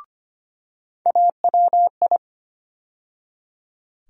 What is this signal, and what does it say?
Morse code, heard as a keyed tone, about 25 wpm, 700 Hz. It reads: AWI
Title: USA Call Sign Postfixes - 3 Letters 25wpm